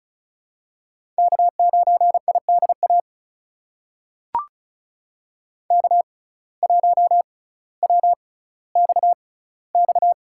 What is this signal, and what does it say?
Morse code, heard as a keyed tone, about 36 wpm, spaced as for 18 wpm, 700 Hz. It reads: K9IDA K1WXX
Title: USA Call Signs - 5 Characters - 6x Character Spacing - 35wpm